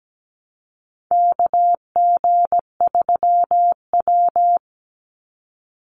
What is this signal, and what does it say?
Morse code, heard as a keyed tone, about 17 wpm, 700 Hz. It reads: KG3W
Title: NA SOTA Chaser Callsigns 17wpm